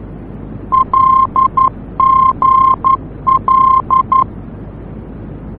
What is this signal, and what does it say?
Morse code, heard as a keyed tone, about 11 wpm, 1060 Hz. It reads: LGL
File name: france / mx / LGL-F-VOR